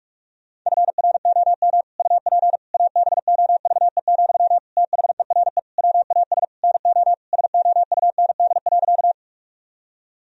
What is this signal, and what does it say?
Morse code, heard as a keyed tone, about 46 wpm, 700 Hz. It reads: FROM UP ABOVE, THERE WAS NO SOUND.
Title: Sentences from Top 1000 Words - Review Part 2 - 45wpm